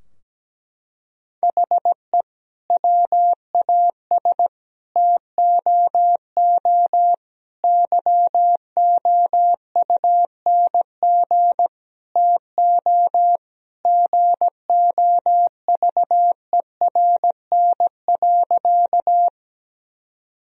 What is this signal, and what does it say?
Morse code, heard as a keyed tone, about 17 wpm, 700 Hz. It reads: HE WAS TOO YOUNG TO GOVERN.
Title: Sentences from Top 1000 Words - Review Part 2 - 17wpm